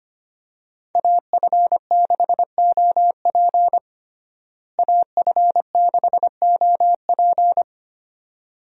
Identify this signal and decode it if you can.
Morse code, heard as a keyed tone, about 25 wpm, 700 Hz. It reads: AF6OP AF6OP
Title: USA Call Signs - 5 Characters - Repeated 2x - 25wpm